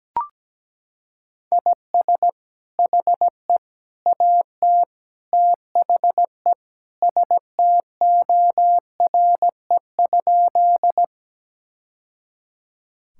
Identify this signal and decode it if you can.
Morse code, heard as a keyed tone, about 17 wpm, 700 Hz. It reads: IS HE AT THE STORE?
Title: Sentences from Top 1000 Words - Review Part 5 - 17wpm